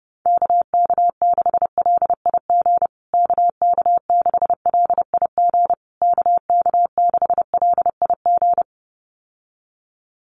Morse code, heard as a keyed tone, about 30 wpm, 700 Hz. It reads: KK6LIG KK6LIG KK6LIG
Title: USA Call Signs Repeated 3x 30wpm